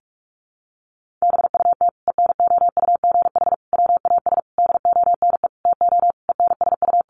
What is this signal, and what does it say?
Morse code, heard as a keyed tone, about 44 wpm, 700 Hz. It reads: BUT ENOUGH WAS DONE TO ENSU
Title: Sentences from Top 1100 Words 45wpm